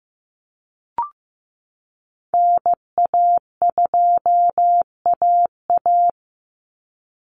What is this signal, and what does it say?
Morse code, heard as a keyed tone, about 15 wpm, 700 Hz. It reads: NA2AA
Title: Call Signs Repeated 2x - 15wpm